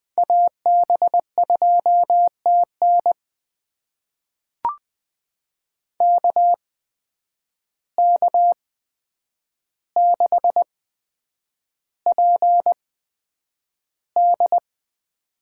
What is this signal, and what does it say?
Morse code, heard as a keyed tone, about 20 wpm, spaced as for 9 wpm, 700 Hz. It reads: AB2TN KK6PD
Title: USA Call Signs - 5 Characters - 8x Character Spacing - 20wpm